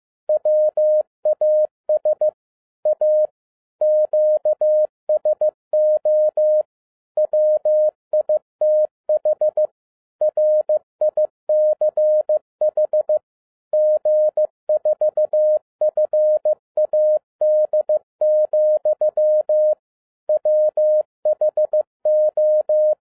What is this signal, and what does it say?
Morse code, heard as a keyed tone, about 15 wpm, 610 Hz. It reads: WAS A QSO WITH RICH G4FAD, WHO
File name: 2011-10-mp3 / n12-JL3YMV 15wpm